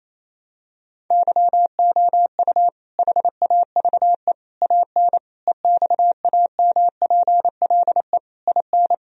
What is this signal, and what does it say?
Morse code, heard as a keyed tone, about 28 wpm, 700 Hz. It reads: YOU HAVE AN EXAMPLE IN
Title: Sentences from Top 1000 Words - Review Part 1 - 28wpm